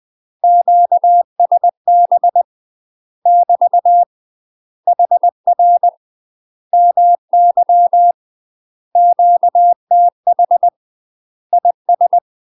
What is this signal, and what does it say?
Morse code, heard as a keyed tone, about 20 wpm, 700 Hz. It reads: QSB <BT> HR MY QTH IS
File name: UK MASTER QSOs TRAINING / 20WPM / UK MASTER QSOs 0070